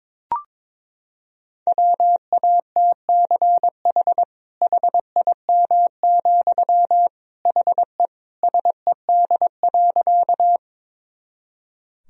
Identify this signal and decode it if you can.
Morse code, heard as a keyed tone, about 22 wpm, 700 Hz. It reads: WATCH HIM, HE SED.
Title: Sentences from Top 300 Words and Abbreviations 22wpm